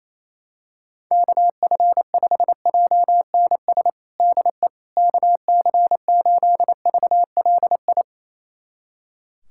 Morse code, heard as a keyed tone, about 28 wpm, 700 Hz. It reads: KF5JNS DE KC8VLI
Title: USA Callsign DE Callsign - 6 Characters - 28wpm